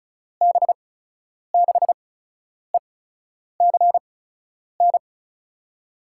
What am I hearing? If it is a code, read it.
Morse code, heard as a keyed tone, about 34 wpm, spaced as for 10 wpm, 700 Hz. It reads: B6ECN
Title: USA Call Signs - 6 Characters - 8x Character Spacing - 35wpm